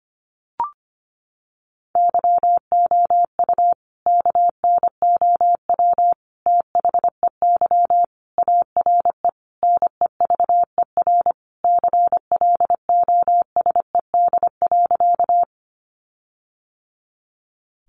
Morse code, heard as a keyed tone, about 25 wpm, 700 Hz. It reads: YOU KNOW THEY ARE NEVER CLOSED.
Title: Sentences from Top 1400 Words 25wpm